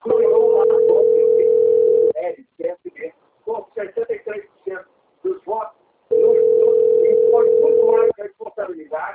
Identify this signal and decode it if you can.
Ringback tone (North America), 440 + 480 Hz, with speech.
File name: out- - - - - .27405